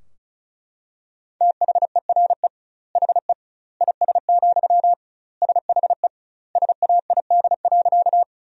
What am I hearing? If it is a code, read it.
Morse code, heard as a keyed tone, about 34 wpm, 700 Hz. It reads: THERE HE IS, SHE SAID.
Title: 2x Word-Spacing - Sentences from Top 100 Words - 35wpm